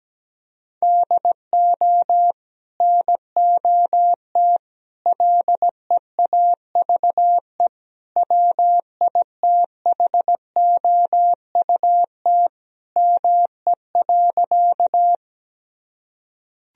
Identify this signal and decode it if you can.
Morse code, heard as a keyed tone, about 17 wpm, 700 Hz. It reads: DO NOT LEAVE WITHOUT ME.
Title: Sentences from Top 1000 Words - Review Part 3 - 17wpm